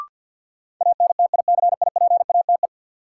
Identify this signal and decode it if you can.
Morse code, heard as a keyed tone, about 53 wpm, 700 Hz. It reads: ANTICIPATE
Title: Sentences from Top 1500 Words - New Word Drill Alt - 50wpm